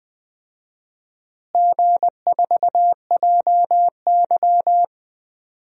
Morse code, heard as a keyed tone, about 20 wpm, 700 Hz. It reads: G4JY
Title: DXCC-100 Call Signs - 4 Characters - Repeated 3x - 20wpm